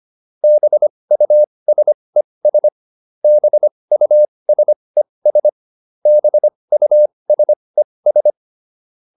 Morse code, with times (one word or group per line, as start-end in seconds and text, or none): 0.4-2.7 BUSES
3.2-5.5 BUSES
6.1-8.3 BUSES